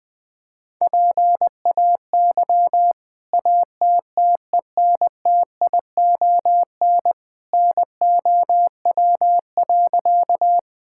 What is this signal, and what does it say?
Morse code, heard as a keyed tone, about 20 wpm, 700 Hz. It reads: PAY ATTENTION NOW.
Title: Sentences from Top 1000 Words - Review Part 6 - 20wpm